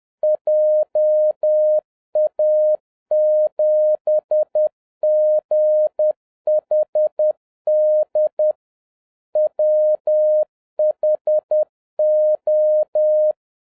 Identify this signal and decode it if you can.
Morse code, heard as a keyed tone, about 10 wpm, 610 Hz. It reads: JA7GHD WHO